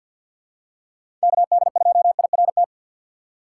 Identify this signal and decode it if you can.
Morse code, heard as a keyed tone, about 48 wpm, 700 Hz. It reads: KD2IRT
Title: USA Call Signs - 6x Character Spacing - 50wpm